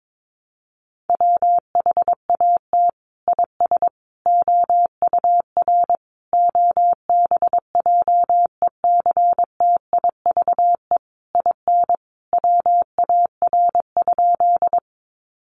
Morse code, heard as a keyed tone, about 22 wpm, 700 Hz. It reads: WHAT IS OUR OBJECTIVE IN WAR?
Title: Sentences from Top 1200 Words 22wpm